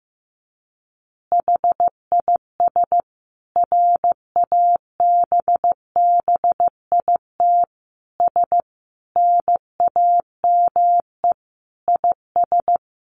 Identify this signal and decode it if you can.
Morse code, heard as a keyed tone, about 15 wpm, 700 Hz. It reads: HIS RABBITS NAME IS
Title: Easy Phrases 15wpm